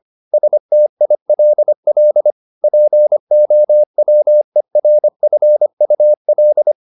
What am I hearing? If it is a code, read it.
Morse code, heard as a keyed tone, about 25 wpm, 600 Hz. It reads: STILL POWERFUL
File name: the old man and the sea part 1 25wpm